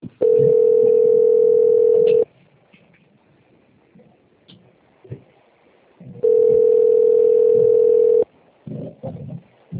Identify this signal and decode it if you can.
Ringback tone (North America), 440 + 480 Hz.